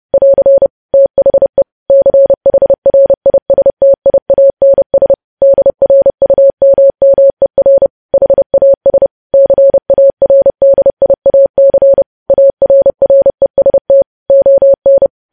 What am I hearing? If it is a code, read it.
Morse code, heard as a keyed tone, about 30 wpm, 550 Hz. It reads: <AR> THE CHRISTIANS DRUMMER HAS CARDIAC ARREST ON
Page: news headlines in morse code at 30 wpm